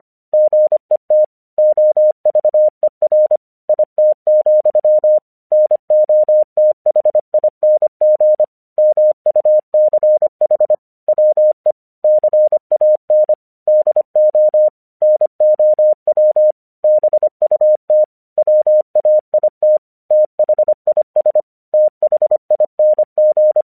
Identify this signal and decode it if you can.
Morse code, heard as a keyed tone, about 25 wpm, 610 Hz. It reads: GET OVER IT, NOTHING MUCH WE CAN DO NOW BUT WAIT THIS THING
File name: 1 English 25wpm / AQC05